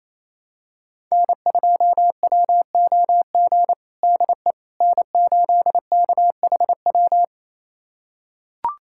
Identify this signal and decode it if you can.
Morse code, heard as a keyed tone, about 28 wpm, 700 Hz. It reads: N2WOG DE N8KHW
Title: USA Callsign DE Callsign - 5 Characters - 28wpm